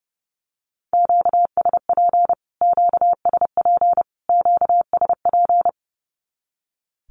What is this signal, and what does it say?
Morse code, heard as a keyed tone, about 30 wpm, 700 Hz. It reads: QSP QSP QSP
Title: Q-Codes Repeated 3x 30wpm